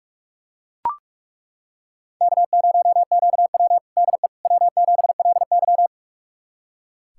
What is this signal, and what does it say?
Morse code, heard as a keyed tone, about 42 wpm, 700 Hz. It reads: K0QW DE W7LY
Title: USA Callsign DE Callsign - 4 Characters - 45wpm